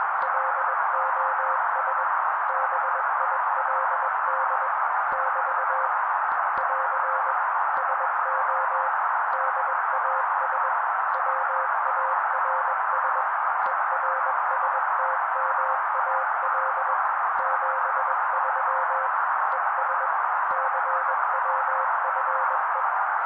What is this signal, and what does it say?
Morse code, heard as a keyed tone, about 21 wpm, 530 Hz. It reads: LOS BILD <BT> <AR> SO DAS WARS ERSTMAL 73 ES CWFE